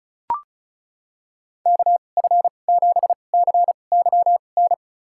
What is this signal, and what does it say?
Morse code, heard as a keyed tone, about 34 wpm, spaced as for 25 wpm, 700 Hz. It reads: KF7CYN
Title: USA Call Signs - 6 Characters - 2x Character Spacing - 35wpm